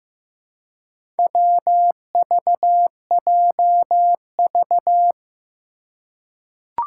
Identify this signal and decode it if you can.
Morse code, heard as a keyed tone, about 15 wpm, 700 Hz. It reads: WVJV
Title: Four Challenging Letter-Number Combinations 15wpm